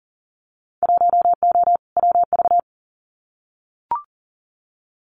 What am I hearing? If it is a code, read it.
Morse code, heard as a keyed tone, about 41 wpm, 700 Hz. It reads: 1O WV
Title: Winter Field Day Exchanges Encore 40wpm